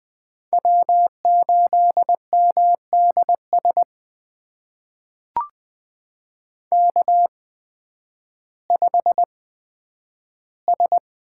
Morse code, heard as a keyed tone, about 20 wpm, spaced as for 11 wpm, 700 Hz. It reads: W8MDS K5S
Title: USA Call Signs - 5 Characters - 8x Character Spacing - 20wpm